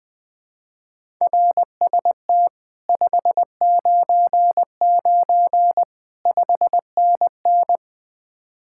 Morse code, heard as a keyed tone, about 20 wpm, 700 Hz. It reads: RST 599 5NN
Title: Speed-Racing Common QSO Phrases - 20wpm